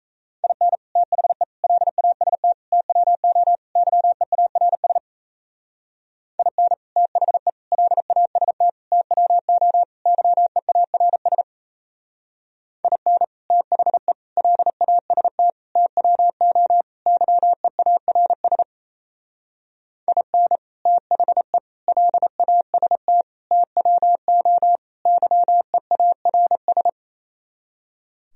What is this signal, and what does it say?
Morse code, heard as a keyed tone, about 36 wpm, 700 Hz. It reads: IN THE LAST TWO YEARS IN THE LAST TWO YEARS IN THE LAST TWO YEARS IN THE LAST TWO YEARS